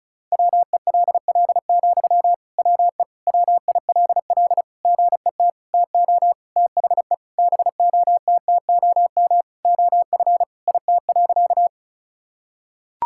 Morse code, heard as a keyed tone, about 36 wpm, 700 Hz. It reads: WELL, WE WILL GET TO THE BOTTOM OF IT.
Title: Sentences from Top 1000 Words - Review Part 6 - 35wpm